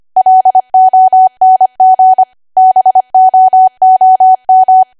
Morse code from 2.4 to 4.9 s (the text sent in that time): BOOM